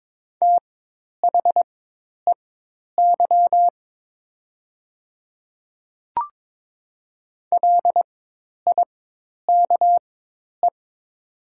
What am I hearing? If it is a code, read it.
Morse code, heard as a keyed tone, about 22 wpm, spaced as for 10 wpm, 700 Hz. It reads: THEY LIKE
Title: 4x Character Spacing - Top 100 Words - 22wpm